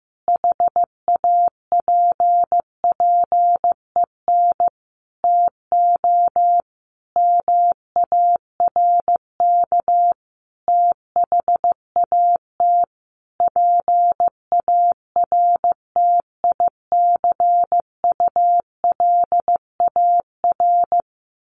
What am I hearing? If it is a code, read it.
Morse code, heard as a keyed tone, about 15 wpm, 700 Hz. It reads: HAPPEN TO MARK THAT PARTICULAR
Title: Sentences from Top 1400 Words 15wpm